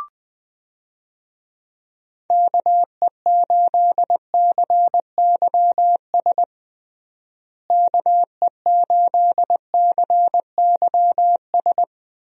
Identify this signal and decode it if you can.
Morse code, heard as a keyed tone, about 20 wpm, 700 Hz. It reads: KE8CYS KE8CYS
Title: USA Call Signs - 6 Characters - Repeated 2x - 20wpm